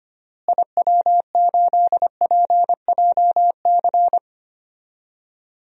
Morse code, heard as a keyed tone, about 25 wpm, 700 Hz. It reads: IW8PJC
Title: DXCC-100 Call Signs - 6 Characters - Repeated 2x - 25wpm